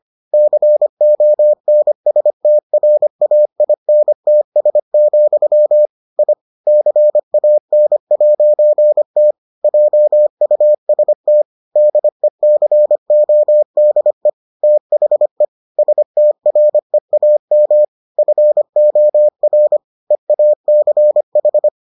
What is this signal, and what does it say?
Morse code, heard as a keyed tone, about 25 wpm, 600 Hz. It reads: CONSTRAINTS, I CAN'T JUST DECODE THE STREAM FOR EACH